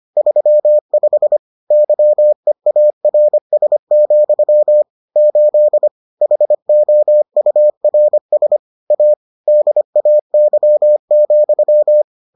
Morse code, heard as a keyed tone, about 25 wpm, 600 Hz. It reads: 35 YEARS, 8 HOURS A DAY,